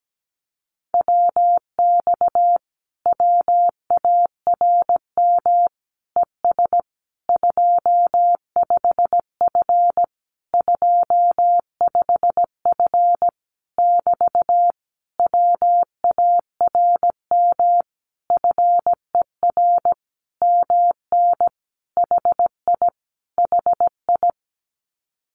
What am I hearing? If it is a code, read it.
Morse code, heard as a keyed tone, about 17 wpm, 700 Hz. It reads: WX WARM ES 25F 25F <BT> WARM FER MN HI HI
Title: Common QSO Phrases Encore 17wpm